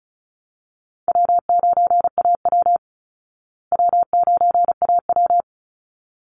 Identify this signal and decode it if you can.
Morse code, heard as a keyed tone, about 34 wpm, 700 Hz. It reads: W9AW W9AW
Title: Call Signs Repeated 3x - 35wpm